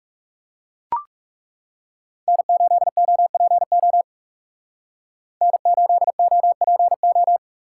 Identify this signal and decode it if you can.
Morse code, heard as a keyed tone, about 36 wpm, 700 Hz. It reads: N8OPO N8OPO
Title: Speed Racing - USA 1x3 Call Signs 30wpm